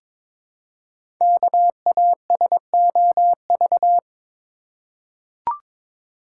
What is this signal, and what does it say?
Morse code, heard as a keyed tone, about 23 wpm, 700 Hz. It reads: KASOV
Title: Code Groups - 5 Letters - 22wpm